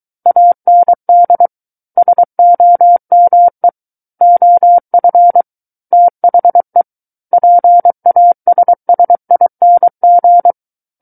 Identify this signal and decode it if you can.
Morse code, heard as a keyed tone, about 23 wpm, 700 Hz. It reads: AND SOME OF THE PASSING
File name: Book One / 14-Chapter 14 - 23 WPM